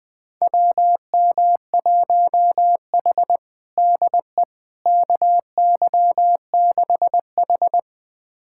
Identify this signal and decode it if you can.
Morse code, heard as a keyed tone, about 20 wpm, 700 Hz. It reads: WM1H DE KY6H
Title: USA Callsign DE Callsign - 4 Characters - 20wpm